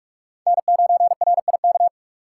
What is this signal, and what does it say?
Morse code, heard as a keyed tone, about 46 wpm, 700 Hz. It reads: N9RIK
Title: USA Call Signs - 5 Characters - 6x Character Spacing - 45wpm